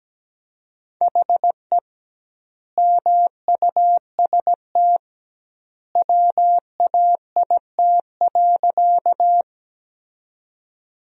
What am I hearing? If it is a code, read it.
Morse code, heard as a keyed tone, about 17 wpm, 700 Hz. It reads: HE MUST WAIT.
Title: 2x Word-Spacing - Sentences from Top 500 Words - 17wpm